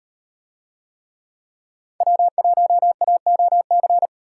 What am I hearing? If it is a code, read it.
Morse code, heard as a keyed tone, about 34 wpm, 700 Hz. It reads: W1AOC